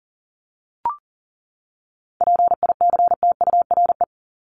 Morse code, heard as a keyed tone, about 38 wpm, 700 Hz. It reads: PICTURE
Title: Top 500 Words 40wpm